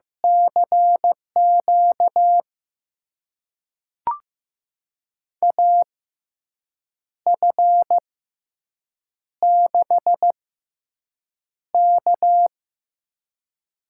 Morse code, heard as a keyed tone, about 16 wpm, spaced as for 6 wpm, 700 Hz. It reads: CQ AF6K
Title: USA Call Signs - 6x Character Spacing - 15wpm